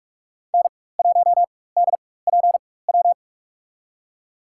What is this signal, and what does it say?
Morse code, heard as a keyed tone, about 46 wpm, spaced as for 21 wpm, 700 Hz. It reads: N1DPW
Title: USA Call Signs - 5 Characters - 4x Character Spacing - 45wpm